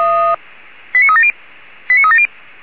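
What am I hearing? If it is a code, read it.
Five-tone selective call (ZVEI), heard as quick runs of five tones, tones of about 70 ms. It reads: 89370 89370
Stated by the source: ZVEI five-tone selective call, code 89360, received on an SDR.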